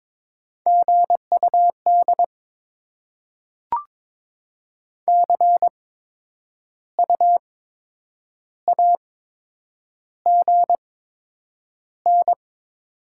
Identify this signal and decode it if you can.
Morse code, heard as a keyed tone, about 22 wpm, spaced as for 8 wpm, 700 Hz. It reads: GUD CU AGN
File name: Top 20 QSO Elements - 8x Character Spacing - 22wpm